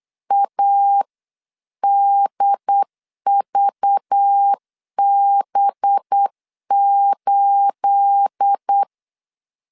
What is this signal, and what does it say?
Morse code, heard as a keyed tone, about 8 wpm, 800 Hz. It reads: ADVB8
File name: CW / DET / 1 9 1500C S B 8wpm 5624